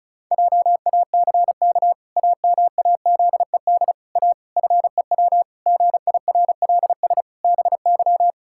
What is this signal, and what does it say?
Morse code, heard as a keyed tone, about 36 wpm, 700 Hz. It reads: JACK AMAZED A FEW GIRLS BY
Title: Pangram Challenge v2 - 35wpm